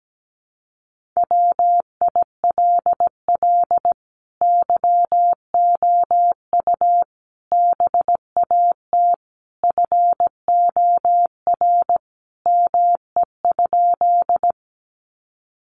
Morse code, heard as a keyed tone, about 17 wpm, 700 Hz. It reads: WILL YOU BAT FOR ME?
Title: Sentences from Top 1500 Words 17wpm